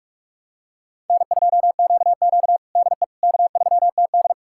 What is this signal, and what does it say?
Morse code, heard as a keyed tone, about 48 wpm, 700 Hz. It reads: N2QQ DE K3TD
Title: USA Callsign DE Callsign - 4 Characters - 45wpm